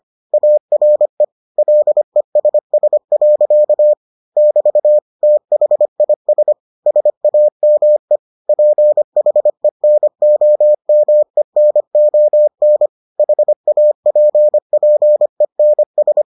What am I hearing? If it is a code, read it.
Morse code, heard as a keyed tone, about 25 wpm, 600 Hz. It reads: ARE LESS. <BT> THIS SAME PHENOMENON HAPPENS